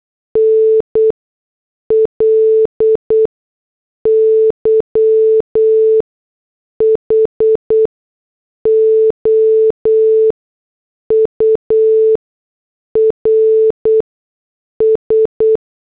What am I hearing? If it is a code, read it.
Morse code, heard as a keyed tone, about 8 wpm, spaced as for 6 wpm, 440 Hz. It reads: NLY HOURS